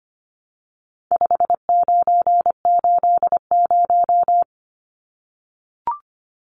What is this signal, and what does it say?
Morse code, heard as a keyed tone, about 25 wpm, 700 Hz. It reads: 5980
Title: Numbers - 4-Digit 25wpm